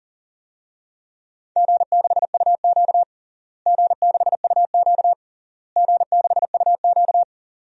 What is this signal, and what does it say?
Morse code, heard as a keyed tone, about 40 wpm, 700 Hz. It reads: G6UQ G6UQ G6UQ
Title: DXCC-100 Call Signs - 4 Characters - Repeated 3x - 40wpm